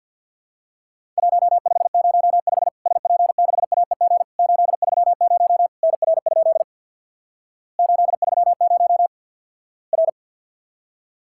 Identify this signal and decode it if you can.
Morse code, heard as a keyed tone, about 50 wpm, 700 Hz. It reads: 1505 SP6AEG 830 NR? 830 R
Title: Worked All Europe DX Contest - QTC Exchange with Repeats - 50wpm